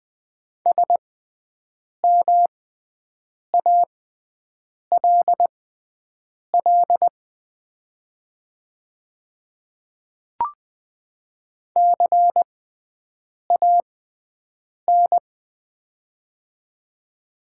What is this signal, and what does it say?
Morse code, heard as a keyed tone, about 20 wpm, spaced as for 7 wpm, 700 Hz. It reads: SMALL CAN
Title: 6x Character Spacing - Top 200 Words - 20wpm